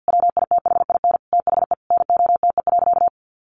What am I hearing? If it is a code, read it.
Morse code, heard as a keyed tone, about 50 wpm, 700 Hz. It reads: WITHIN THE NOTE.
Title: Sentences from Top 1500 Words 50wpm